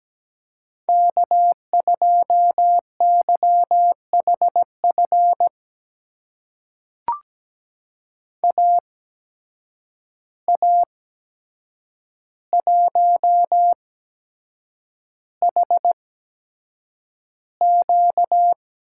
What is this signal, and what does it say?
Morse code, heard as a keyed tone, about 17 wpm, spaced as for 7 wpm, 700 Hz. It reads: K2YHF AA1HQ
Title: USA Call Signs - 5 Characters - 8x Character Spacing - 17wpm